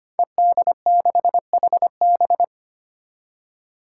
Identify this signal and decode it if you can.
Morse code, heard as a keyed tone, about 25 wpm, 700 Hz. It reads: ED6HB
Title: Five Challenging Letter-Number Combinations 25wpm